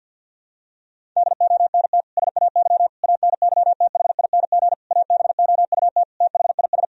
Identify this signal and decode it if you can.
Morse code, heard as a keyed tone, about 47 wpm, 700 Hz. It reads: DONT SAY ANYTHING ABOUT THIS
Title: Sentences from Top 1000 Words - Review Part 1 - 50wpm